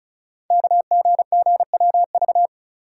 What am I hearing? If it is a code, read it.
Morse code, heard as a keyed tone, about 36 wpm, 700 Hz. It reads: KGGWV
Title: Code Groups - 5 Letters - 35wpm